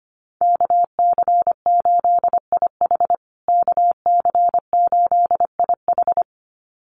Morse code, heard as a keyed tone, about 25 wpm, 700 Hz. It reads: KC8IH KC8IH
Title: USA Call Signs Repeated 2x 25wpm